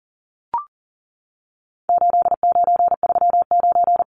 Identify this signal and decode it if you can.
Morse code, heard as a keyed tone, about 40 wpm, 700 Hz. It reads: 8939
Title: Numbers - 4-Digit 40wpm